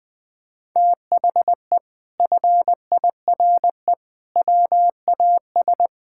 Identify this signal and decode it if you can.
Morse code, heard as a keyed tone, about 21 wpm, 700 Hz. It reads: THE FIRE WAS
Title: Sentences from Top 1000 Words - Review Part 6 - 20wpm